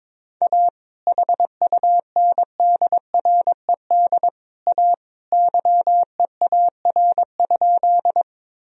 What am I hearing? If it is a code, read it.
Morse code, heard as a keyed tone, about 21 wpm, 700 Hz. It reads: A HUNDRED A YEAR?
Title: Sentences from Top 1000 Words - Review Part 2 - 22wpm